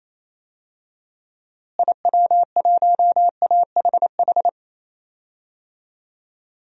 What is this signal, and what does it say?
Morse code, heard as a keyed tone, about 27 wpm, 700 Hz. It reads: IW1AHH
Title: DXCC-100 Call Signs - 6 Characters - Repeated 2x - 28wpm